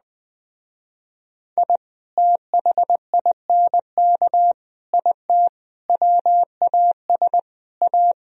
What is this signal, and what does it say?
Morse code, heard as a keyed tone, about 19 wpm, 700 Hz. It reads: I THINK IT WAS A
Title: Speed-Racing - Sets of 5 Words 17wpm